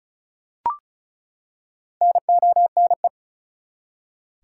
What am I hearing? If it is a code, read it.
Morse code, heard as a keyed tone, about 36 wpm, 700 Hz. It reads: NONE
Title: Four Letter Words - 35wpm